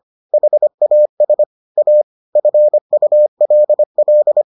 Morse code, heard as a keyed tone, about 25 wpm, 600 Hz. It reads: HAS A FULL